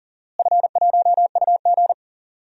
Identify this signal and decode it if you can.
Morse code, heard as a keyed tone, about 40 wpm, 700 Hz. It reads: F1UG
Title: DXCC-100 Call Signs - 4 Characters - Repeated 3x - 40wpm